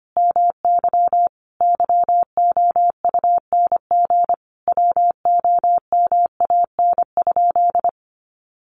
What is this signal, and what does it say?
Morse code, heard as a keyed tone, about 25 wpm, 700 Hz. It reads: MY YOUNG WOMAN?
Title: Sentences from Top 1100 Words 25wpm